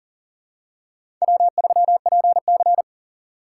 Morse code, heard as a keyed tone, about 40 wpm, 700 Hz. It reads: W3PC
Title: USA Call Signs - 4 Characters - 6x Character Spacing - 40wpm